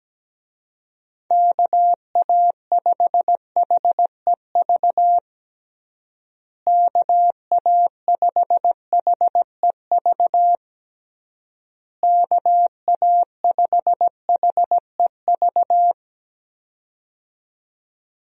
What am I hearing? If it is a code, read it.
Morse code, heard as a keyed tone, about 17 wpm, 700 Hz. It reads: KA5HEV KA5HEV KA5HEV
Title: USA Call Signs - 6 Characters - Repeated 3x - 17wpm